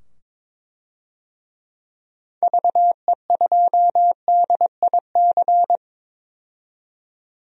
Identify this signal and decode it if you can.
Morse code, heard as a keyed tone, about 22 wpm, 700 Hz. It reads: VE2DIC
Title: DXCC-100 Call Signs - 6 Characters - 22wpm